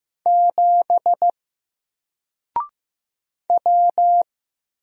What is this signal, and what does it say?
Morse code, heard as a keyed tone, about 15 wpm, 700 Hz. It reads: 7 W
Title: Single Letter-Number Alt 15wpm